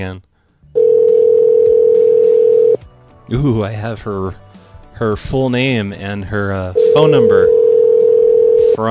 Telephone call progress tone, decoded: ringback tone